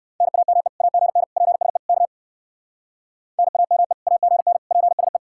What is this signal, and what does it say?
Morse code, heard as a keyed tone, about 56 wpm, 700 Hz. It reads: NAME AGN PSE K NAME AGN PSE
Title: Speed-Racing Common QSO Phrases Encore - 45wpm